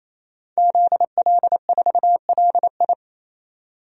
Morse code, heard as a keyed tone, about 28 wpm, 700 Hz. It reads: ZL4LI
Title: DXCC-100 Call Signs - 5 Characters - 28wpm